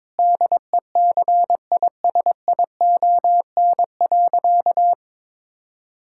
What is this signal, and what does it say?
Morse code, heard as a keyed tone, about 22 wpm, 700 Hz. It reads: DECISION.
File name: practice / 22wpm / Sentences with 8-Word Phrases - 22wpm